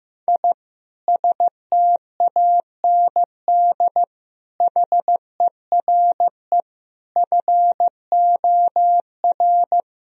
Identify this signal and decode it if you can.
Morse code, heard as a keyed tone, about 16 wpm, 700 Hz. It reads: I STAND HERE FOR
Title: Sentences from Top 1000 Words - Review Part 3 - 15wpm